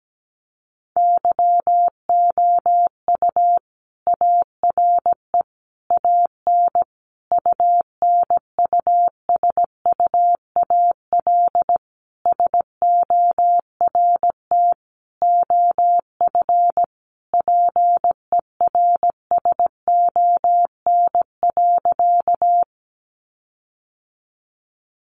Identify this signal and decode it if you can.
Morse code, heard as a keyed tone, about 17 wpm, 700 Hz. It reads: YOU ARE AN UNUSUAL SORT OF PERSON.
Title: Sentences from Top 1400 Words 17wpm